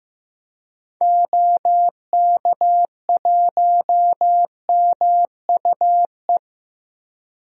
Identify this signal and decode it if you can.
Morse code, heard as a keyed tone, about 15 wpm, 700 Hz. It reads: OK1MUE
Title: DXCC-100 Call Signs - 6 Characters - 15wpm